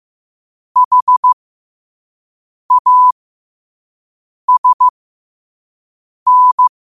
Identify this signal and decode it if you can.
Morse code, heard as a keyed tone, about 15 wpm, spaced as for 5 wpm, 1000 Hz. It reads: HASN
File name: qotdfiles / 5wpm